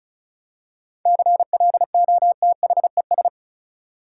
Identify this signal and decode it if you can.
Morse code, heard as a keyed tone, about 34 wpm, 700 Hz. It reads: CLOTHES
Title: Seven Letter Words - 35wpm